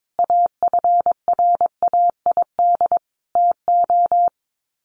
Morse code, heard as a keyed tone, about 22 wpm, 700 Hz. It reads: AFRAID TO
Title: Sets of 5 Words Encore2 22wpm